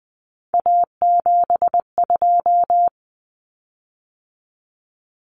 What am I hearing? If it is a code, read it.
Morse code, heard as a keyed tone, about 20 wpm, 700 Hz. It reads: A72
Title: Common Cut Numbers - 3-Digit 20wpm